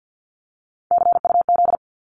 Morse code, heard as a keyed tone, about 50 wpm, 700 Hz. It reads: CUZ
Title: Abbreviations 50wpm